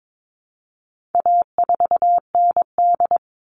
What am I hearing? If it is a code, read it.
Morse code, heard as a keyed tone, about 22 wpm, 700 Hz. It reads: A4ND
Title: All Cut Numbers - 4-Digit 22wpm